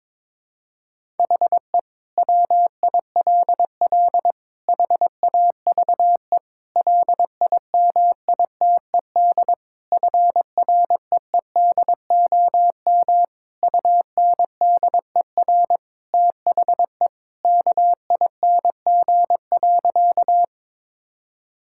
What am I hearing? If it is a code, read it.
Morse code, heard as a keyed tone, about 23 wpm, 700 Hz. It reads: HE WILL HAVE LIMITED FREEDOM UNDER THE KING.
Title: Sentences from Top 1000 Words - Review Part 6 - 22wpm